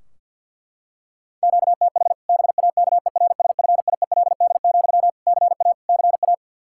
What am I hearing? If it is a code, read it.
Morse code, heard as a keyed tone, about 46 wpm, 700 Hz. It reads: QTH BAKERSFIELD, CA CA
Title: Practice QSO - Exchange 1 - Beginning - 50wpm